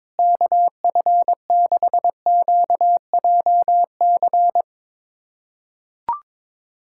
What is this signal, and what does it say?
Morse code, heard as a keyed tone, about 22 wpm, 700 Hz. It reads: KF6QJC
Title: USA Callsign DE Callsign 22wpm